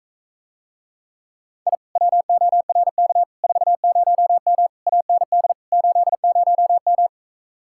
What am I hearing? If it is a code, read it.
Morse code, heard as a keyed tone, about 44 wpm, 700 Hz. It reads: I WORK 40M AND 80M